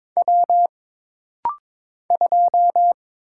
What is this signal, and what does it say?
Morse code, heard as a keyed tone, about 22 wpm, 700 Hz. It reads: W 2
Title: Single Letter-Number - Rapid-Fire 22wpm